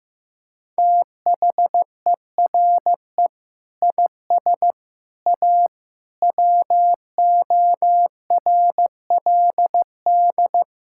Morse code, heard as a keyed tone, about 15 wpm, 700 Hz. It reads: THERE IS A WORLD
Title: Sentences from Top 1600 Words 15wpm